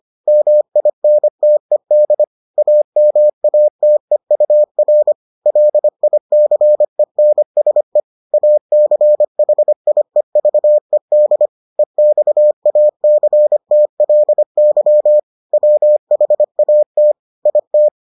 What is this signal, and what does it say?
Morse code, heard as a keyed tone, about 25 wpm, 600 Hz. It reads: MINTED AMATEUR LICENSE ACHIEVED EXACTLY WHAT IT